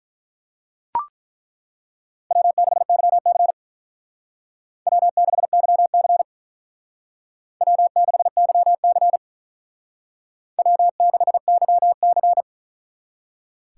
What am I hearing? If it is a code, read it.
Morse code, heard as a keyed tone, about 42 wpm, 700 Hz. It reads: W6YC W6YC W6YC W6YC
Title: Speed Racing - USA 1x2 Call Signs 35wpm